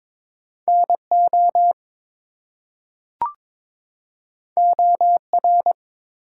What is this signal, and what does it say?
Morse code, heard as a keyed tone, about 22 wpm, 700 Hz. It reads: NO OR
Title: Two Letter Words - 22wpm